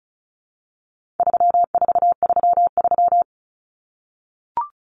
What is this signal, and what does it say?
Morse code, heard as a keyed tone, about 34 wpm, 700 Hz. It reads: 3433
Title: Numbers - 4-Digit 35wpm